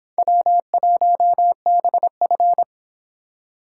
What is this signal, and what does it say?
Morse code, heard as a keyed tone, about 26 wpm, 700 Hz. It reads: W1BF